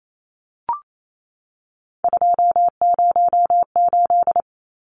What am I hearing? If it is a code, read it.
Morse code, heard as a keyed tone, about 28 wpm, 700 Hz. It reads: 208
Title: Numbers - 3-Digit 28wpm